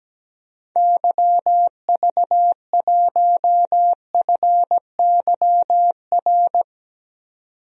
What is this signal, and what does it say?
Morse code, heard as a keyed tone, about 17 wpm, 700 Hz. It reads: YV1FYR
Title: DXCC-100 Call Signs - 6 Characters - 17wpm